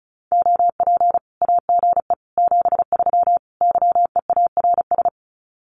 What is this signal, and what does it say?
Morse code, heard as a keyed tone, about 34 wpm, 700 Hz. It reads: OP AGE 73 YEARS
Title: Common QSO Phrases 35wpm